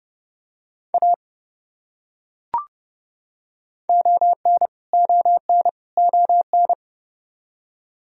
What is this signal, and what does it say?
Morse code, heard as a keyed tone, about 31 wpm, 700 Hz. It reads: A ON ON ON
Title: Top 100 Words Repeated 30wpm